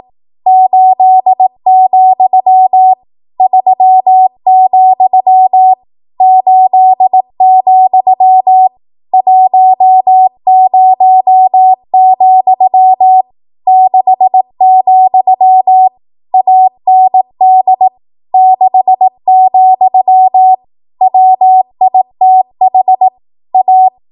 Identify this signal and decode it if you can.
Morse code, heard as a keyed tone, about 18 wpm, 750 Hz. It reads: 8, 3, 8, 10, 6, AND 6, WITH A
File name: Archive / 18 WPM / 18WPM